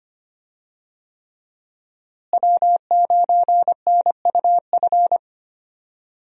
Morse code, heard as a keyed tone, about 25 wpm, 700 Hz. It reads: W9NUF
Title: USA Call Signs - 5 Characters - Repeated 3x - 25wpm